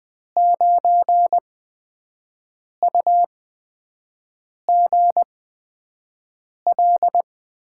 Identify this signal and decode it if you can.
Morse code, heard as a keyed tone, about 20 wpm, spaced as for 5 wpm, 700 Hz. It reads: 9UGL
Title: USA Call Signs - 8x Character Spacing - 20wpm